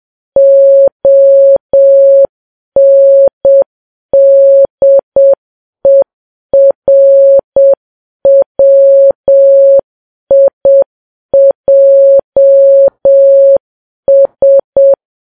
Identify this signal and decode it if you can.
Morse code, heard as a keyed tone, about 7 wpm, 550 Hz. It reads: ONDERWIJS
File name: Files Web tls / Morse on9fv / morse7kps